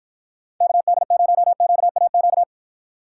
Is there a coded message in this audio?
Yes: Morse code from 0.6 to 2.4 s, heard as a keyed tone, about 52 wpm, 700 Hz. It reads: KD0QAQ